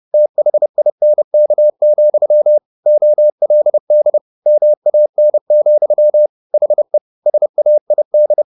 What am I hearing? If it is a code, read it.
Morse code, heard as a keyed tone, about 30 wpm, 600 Hz. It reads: THINK, OLD MAN, HE SAID
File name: the old man and the sea / the old man and the sea 1 30wpm